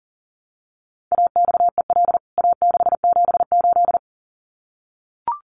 Morse code, heard as a keyed tone, about 40 wpm, 700 Hz. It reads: AXEL A678
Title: CWT Exchange 40wpm